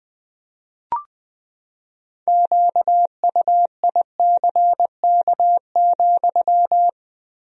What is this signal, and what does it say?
Morse code, heard as a keyed tone, about 20 wpm, 700 Hz. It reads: QUICK,
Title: 3x Word-Spacing - Sentences from Top 500 Words - 20wpm